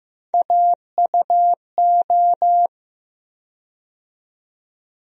Morse code, heard as a keyed tone, about 15 wpm, 700 Hz. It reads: AUO
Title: USA Call Sign Postfixes - 3 Letters 15wpm